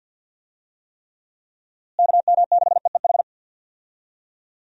Morse code, heard as a keyed tone, about 47 wpm, 700 Hz. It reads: KM6EEH
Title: USA Call Signs - 6 Characters - Repeated 2x - 50wpm